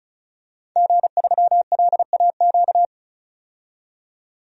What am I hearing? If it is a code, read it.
Morse code, heard as a keyed tone, about 34 wpm, 700 Hz. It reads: G3LAQ
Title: DXCC-100 Call Signs - 5 Characters - 35wpm